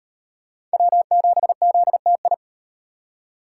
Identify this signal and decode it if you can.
Morse code, heard as a keyed tone, about 38 wpm, 700 Hz. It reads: W7ZTI